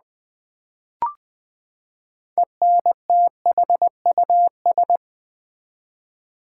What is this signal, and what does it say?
Morse code, heard as a keyed tone, about 20 wpm, 700 Hz. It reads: ENTHUS
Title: Build A Word - 6 Letters At A Time - 12-Letter Words - 20wpm